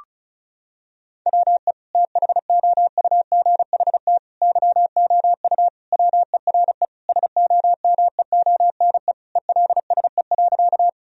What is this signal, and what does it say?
Morse code, heard as a keyed tone, about 36 wpm, 700 Hz. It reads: WE THOUGHT YOU WERE SOMEONE ELSE.
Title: Sentences from Top 1000 Words - Review Part 4 - 35wpm